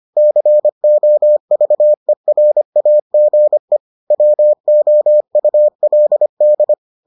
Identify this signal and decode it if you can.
Morse code, heard as a keyed tone, about 25 wpm, 600 Hz. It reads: COVERAGE WOULD